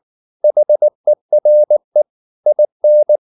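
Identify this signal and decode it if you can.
Morse code, heard as a keyed tone, about 19 wpm, 600 Hz. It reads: HERE IN
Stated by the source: Morse code at 19 wpm from CW med Gnister, reading Ken Follett's novel Never in English.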